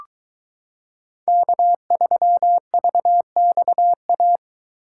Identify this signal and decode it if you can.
Morse code, heard as a keyed tone, about 22 wpm, 700 Hz. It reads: K3VXA